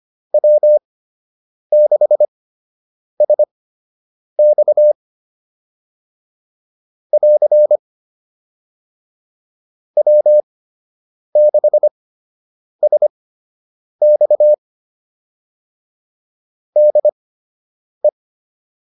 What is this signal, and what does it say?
Morse code, heard as a keyed tone, about 25 wpm, spaced as for 8 wpm, 600 Hz. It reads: W6SX <AR> W6SX DE